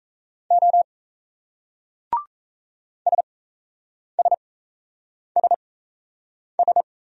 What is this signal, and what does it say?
Morse code, heard as a keyed tone, about 37 wpm, spaced as for 8 wpm, 700 Hz. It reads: O SSSS